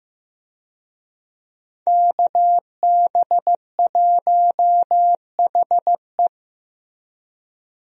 Morse code, heard as a keyed tone, about 15 wpm, 700 Hz. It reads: KB1HE
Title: USA Call Signs - 5 Characters - Repeated 3x - 15wpm